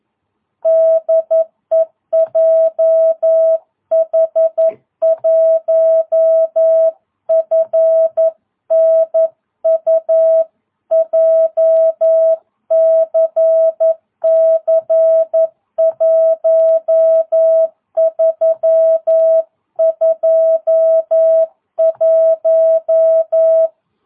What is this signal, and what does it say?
Morse code, heard as a keyed tone, about 11 wpm, 650 Hz. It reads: DE JH1FNU JCC 1321
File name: jh8cbh / 2022cw